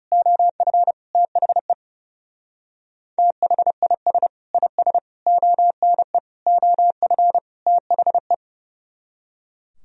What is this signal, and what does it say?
Morse code, heard as a keyed tone, about 39 wpm, 700 Hz. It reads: OF THE THIS IS ONE OF THE